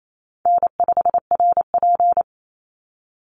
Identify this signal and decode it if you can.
Morse code, heard as a keyed tone, about 28 wpm, 700 Hz. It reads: N5RP
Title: Easy Call Signs - 28wpm